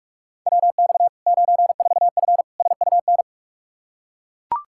Morse code, heard as a keyed tone, about 42 wpm, 700 Hz. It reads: WX 94F SUN
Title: Practice QSO - Exchange 2 - Info - 45wpm